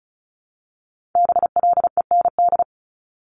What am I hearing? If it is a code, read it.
Morse code, heard as a keyed tone, about 36 wpm, 700 Hz. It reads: BLEND
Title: Multi-letter Phonemes Repeated 3x 35wpm